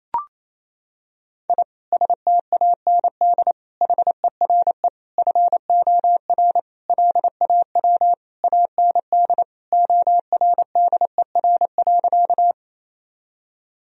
Morse code, heard as a keyed tone, about 28 wpm, 700 Hz. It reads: I STAND HERE FOR LAW AND ORDER.
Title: Sentences from Top 1000 Words - Review Part 3 - 28wpm